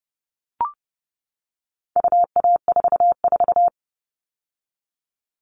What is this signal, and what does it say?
Morse code, heard as a keyed tone, about 30 wpm, 700 Hz. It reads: UA44
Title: All Cut Numbers - 4-Digit 30wpm